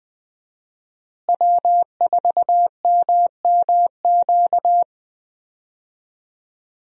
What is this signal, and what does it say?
Morse code, heard as a keyed tone, about 20 wpm, 700 Hz. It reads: W4MMQ
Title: USA Call Signs - 5 Characters - Repeated 3x - 20wpm